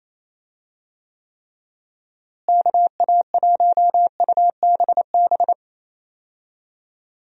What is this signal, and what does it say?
Morse code, heard as a keyed tone, about 28 wpm, 700 Hz. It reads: KA1UBB
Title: USA Call Signs - 6 Characters - Repeated 2x - 28wpm